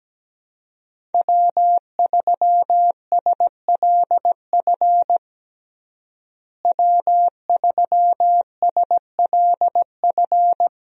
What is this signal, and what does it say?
Morse code, heard as a keyed tone, about 17 wpm, 700 Hz. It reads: W3SLF W3SLF
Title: USA Call Signs - 5 Characters - Repeated 3x - 17wpm